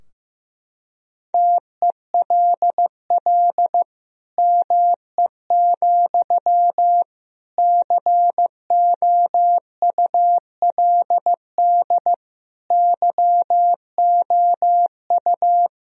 Morse code, heard as a keyed tone, about 16 wpm, 700 Hz. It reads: TELL ME, COULD YOU
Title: Sentences from Top 1000 Words - Review Part 2 - 15wpm